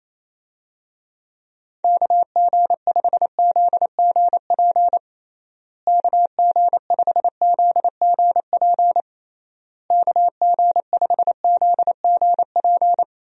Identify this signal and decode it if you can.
Morse code, heard as a keyed tone, about 27 wpm, 700 Hz. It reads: KG5ZGP KG5ZGP KG5ZGP
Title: USA Call Signs - 6 Characters - Repeated 3x - 28wpm